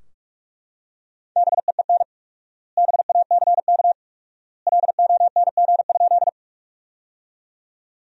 Morse code, heard as a keyed tone, about 42 wpm, 700 Hz. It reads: BEEN BACK LONG?
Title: 4x Word-Spacing - Sentences from Top 200 Words - 45wpm